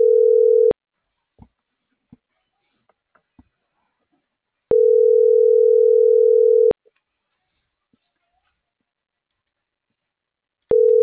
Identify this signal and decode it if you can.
Ringback tone (North America), 440 + 480 Hz.
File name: CENTRAL NMP / out erasto - .5929